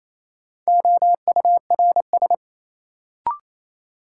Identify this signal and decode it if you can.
Morse code, heard as a keyed tone, about 27 wpm, 700 Hz. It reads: OURS
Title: Sentences from Top 1300 Words - New Word Drill Alt - 28wpm